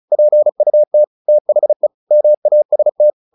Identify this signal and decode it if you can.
Morse code, heard as a keyed tone, about 34 wpm, 600 Hz. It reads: PUT THE MAST
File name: the old man and the sea 2 35wpm